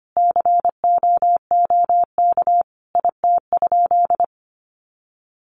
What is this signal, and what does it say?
Morse code, heard as a keyed tone, about 25 wpm, 700 Hz. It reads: COOK IT?
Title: Sentences from Top 1500 Words 25wpm